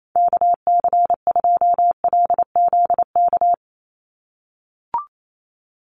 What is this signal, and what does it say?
Morse code, heard as a keyed tone, about 28 wpm, 700 Hz. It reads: KC2LZK
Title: USA Call Signs 28wpm